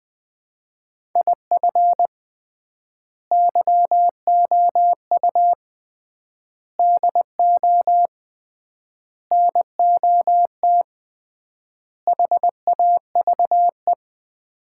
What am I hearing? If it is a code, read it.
Morse code, heard as a keyed tone, about 20 wpm, 700 Hz. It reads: IF YOU DO NOT HAVE
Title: 3x Word-Spacing - Sets of 5 Words - 20wpm